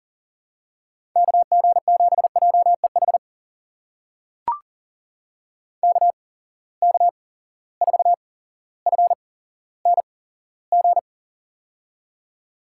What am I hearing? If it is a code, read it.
Morse code, heard as a keyed tone, about 39 wpm, spaced as for 17 wpm, 700 Hz. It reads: KG7JEH KK4FNG
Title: USA Call Signs - 6 Characters - 8x Character Spacing - 40wpm